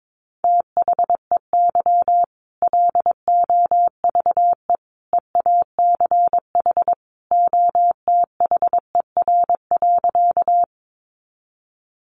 Morse code, heard as a keyed tone, about 22 wpm, 700 Hz. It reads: THEY LOVE EACH OTHER.
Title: Easy Phrases 22wpm